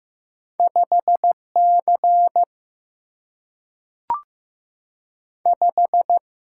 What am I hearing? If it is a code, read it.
Morse code, heard as a keyed tone, about 15 wpm, 700 Hz. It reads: 5 C 5
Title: Single-Letter Single-Number 15wpm